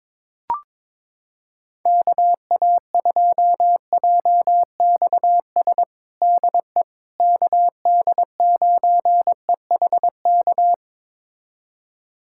Morse code, heard as a keyed tone, about 22 wpm, 700 Hz. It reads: KA2JXS DE KD9EHK
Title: USA Callsign DE Callsign - 6 Characters - 22wpm